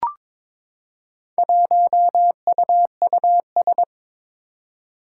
Morse code, heard as a keyed tone, about 22 wpm, 700 Hz. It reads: 1UUS
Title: Four Challenging Letter-Number Combinations 22wpm